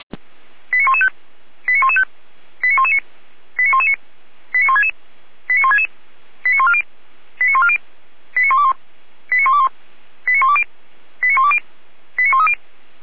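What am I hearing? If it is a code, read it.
Five-tone selective call (ZVEI), heard as quick runs of five tones, tones of about 80 ms. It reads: 89115 89115 89119 89119 89166 89166 89140 89140 89121 89121 89120 89120 89130
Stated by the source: ZVEI five-tone selective call 89092 received by SDR.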